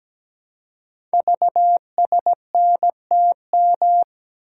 Morse code, heard as a keyed tone, about 17 wpm, 700 Hz. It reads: VSNTM
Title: Code Groups - 5 Letter:Numbers - 17wpm